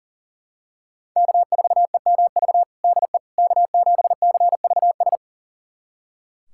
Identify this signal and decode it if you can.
Morse code, heard as a keyed tone, about 40 wpm, 700 Hz. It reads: K4EMV DE K7CVS
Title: USA Callsign DE Callsign - 5 Characters - 40wpm